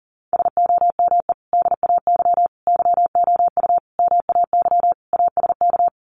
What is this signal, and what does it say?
Morse code, heard as a keyed tone, about 40 wpm, 700 Hz. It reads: SOME DAY YOU MAY ASK
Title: Sentences from Top 200 Words 40wpm